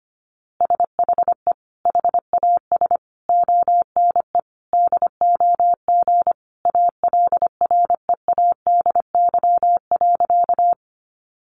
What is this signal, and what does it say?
Morse code, heard as a keyed tone, about 25 wpm, 700 Hz. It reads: SHE HAS ONE DOG ALREADY.
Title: Easy Phrases 25wpm